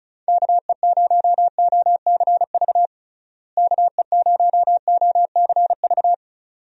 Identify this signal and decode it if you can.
Morse code, heard as a keyed tone, about 36 wpm, 700 Hz. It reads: KE0OCV KE0OCV
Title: USA Call Signs - 6 Characters - Repeated 2x - 35wpm